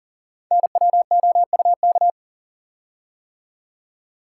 Morse code, heard as a keyed tone, about 39 wpm, 700 Hz. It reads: NWOUK
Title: Code Groups - 5 Letters - 40wpm